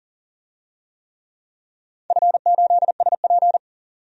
Morse code, heard as a keyed tone, about 44 wpm, 690 Hz. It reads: F8SP
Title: DXCC-100 Call Signs - 4 Characters - 40wpm